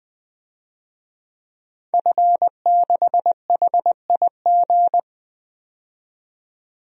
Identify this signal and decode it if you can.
Morse code, heard as a keyed tone, about 21 wpm, 700 Hz. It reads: F6HIG
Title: DXCC-100 Call Signs - 5 Characters - 20wpm